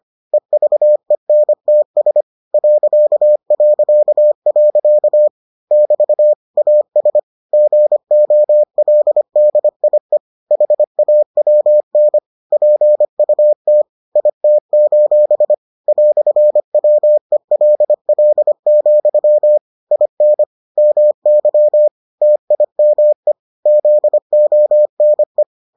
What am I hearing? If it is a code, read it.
Morse code, heard as a keyed tone, about 25 wpm, 600 Hz. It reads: EVENTS ... <BT> AS GOLDIE HAWN PUT IT: "WELL, IN MY TIME ZONE